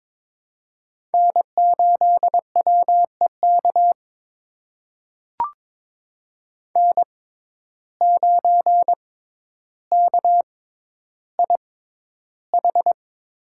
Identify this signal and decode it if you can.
Morse code, heard as a keyed tone, about 21 wpm, spaced as for 11 wpm, 700 Hz. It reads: N8WEK N9KIH
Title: USA Call Signs - 5 Characters - 6x Character Spacing - 22wpm